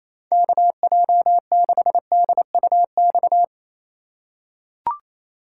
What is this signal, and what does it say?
Morse code, heard as a keyed tone, about 27 wpm, 700 Hz. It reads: KJ6DUX
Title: USA Call Signs - 6 Characters - 4x Character Spacing - 28wpm